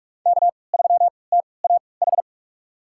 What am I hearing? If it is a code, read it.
Morse code, heard as a keyed tone, about 46 wpm, spaced as for 25 wpm, 700 Hz. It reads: K3TAH
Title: USA Call Signs - 3x Character Spacing - 45wpm